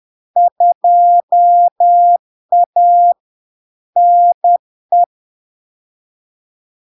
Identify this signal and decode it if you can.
Morse code, heard as a keyed tone, about 10 wpm, 700 Hz. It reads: 2A NE
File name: LARK-FD-2022-Log-in-CW-10wpm-700-Hz